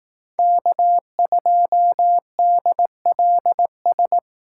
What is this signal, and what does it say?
Morse code, heard as a keyed tone, about 18 wpm, 700 Hz. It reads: K2DLS
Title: Speed Racing - USA 1x3 Call Signs 15wpm